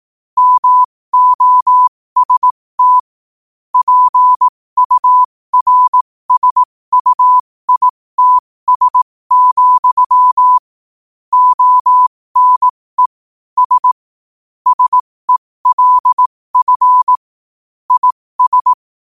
Morse code, heard as a keyed tone, about 18 wpm, spaced as for 15 wpm, 1000 Hz. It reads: MOST PURSUITS, ONE S SELF IS
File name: qotdfiles / 15wpm